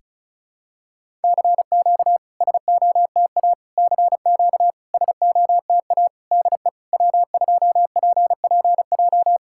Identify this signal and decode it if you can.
Morse code, heard as a keyed tone, about 34 wpm, 660 Hz. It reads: CQ SOTA CQ SOTA DE W2PPJ
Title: SOTA QSO Practice 35wpm